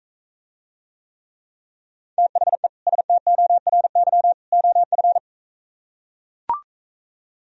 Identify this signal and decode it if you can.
Morse code, heard as a keyed tone, about 41 wpm, 700 Hz. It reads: THE STORY OF